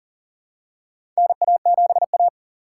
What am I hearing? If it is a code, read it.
Morse code, heard as a keyed tone, about 40 wpm, 700 Hz. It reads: NA7A
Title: USA Call Signs - 4 Characters - 3x Character Spacing - 40wpm